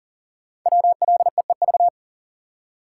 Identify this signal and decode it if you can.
Morse code, heard as a keyed tone, about 38 wpm, 700 Hz. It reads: WLEEV
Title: Code Groups - 5 Letters - 40wpm